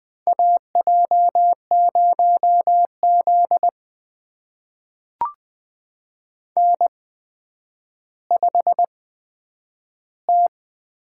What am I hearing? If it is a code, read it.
Morse code, heard as a keyed tone, about 21 wpm, spaced as for 10 wpm, 700 Hz. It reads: AJ0Z N5T
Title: USA Call Signs - 4 Characters - 8x Character Spacing - 20wpm